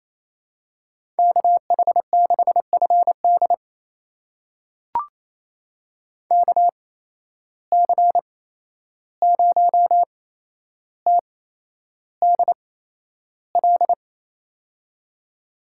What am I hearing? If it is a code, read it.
Morse code, heard as a keyed tone, about 28 wpm, spaced as for 11 wpm, 700 Hz. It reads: KH6FD KC0TDL
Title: USA Call Signs - 8x Character Spacing - 28wpm